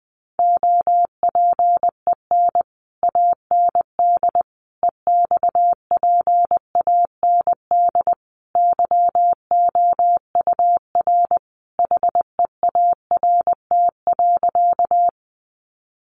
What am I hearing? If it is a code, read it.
Morse code, heard as a keyed tone, about 20 wpm, 700 Hz. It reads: OPEN AND EXPAND YOUR HEART.
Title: Sentences from Top 1400 Words 20wpm